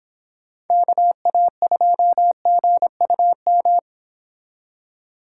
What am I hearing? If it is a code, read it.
Morse code, heard as a keyed tone, about 30 wpm, 700 Hz. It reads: KA2GUM